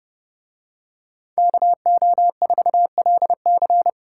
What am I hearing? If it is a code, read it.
Morse code, heard as a keyed tone, about 27 wpm, 700 Hz. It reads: KO4LC